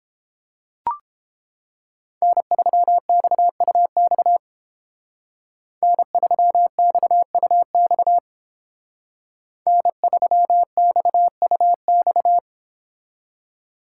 Morse code, heard as a keyed tone, about 29 wpm, 700 Hz. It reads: N3XUX N3XUX N3XUX